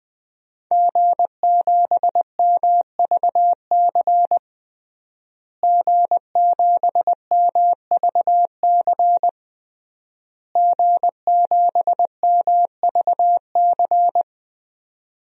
Morse code, heard as a keyed tone, about 20 wpm, 700 Hz. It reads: G7MVC G7MVC G7MVC
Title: DXCC-100 Call Signs - 5 Characters - Repeated 3x - 20wpm